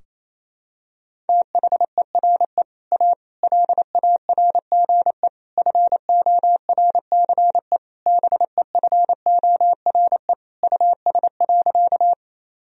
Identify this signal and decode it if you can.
Morse code, heard as a keyed tone, about 28 wpm, 700 Hz. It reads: THERE A LARGE FORCE BEFORE US.
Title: Sentences from Top 1000 Words - Review Part 3 - 28wpm